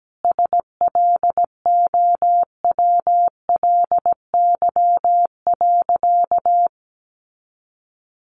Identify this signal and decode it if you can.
Morse code, heard as a keyed tone, about 17 wpm, 700 Hz. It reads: SLOWLY.
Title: Sentences from Top 1100 Words 17wpm